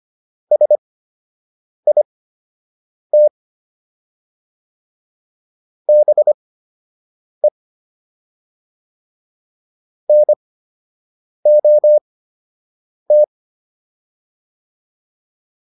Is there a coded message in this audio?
Yes: Morse code from 0.5 to 13.2 s, heard as a keyed tone, about 25 wpm, spaced as for 7 wpm, 600 Hz. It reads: SIT BE NOT